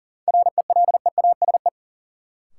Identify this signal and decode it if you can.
Morse code, heard as a keyed tone, about 40 wpm, 700 Hz. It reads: RELEASE
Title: Sentences from Top 1100 Words - New Word Drill Alt - 40wpm